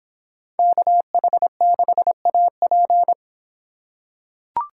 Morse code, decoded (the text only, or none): KH6AP